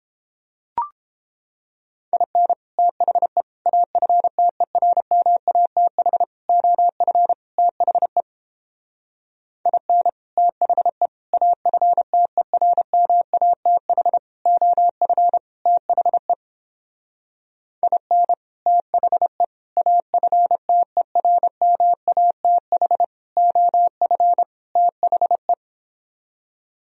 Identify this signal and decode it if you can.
Morse code, heard as a keyed tone, about 29 wpm, 700 Hz. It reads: IN THE AFTERMATH OF THE IN THE AFTERMATH OF THE IN THE AFTERMATH OF THE